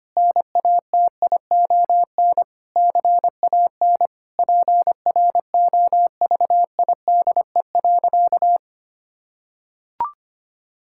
Morse code, heard as a keyed tone, about 25 wpm, 700 Hz. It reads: NATION CAN PROVIDE.
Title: Sentences from Top 1000 Words - Review Part 6 - 25wpm